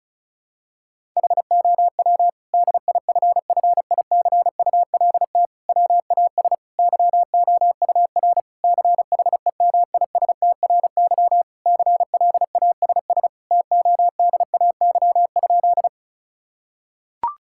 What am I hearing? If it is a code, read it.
Morse code, heard as a keyed tone, about 36 wpm, 700 Hz. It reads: HOW DIFFICULT WAS YOUR CHEMISTRY CLASS TODAY?
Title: Sentences from Top 1000 Words - Review Part 6 - 35wpm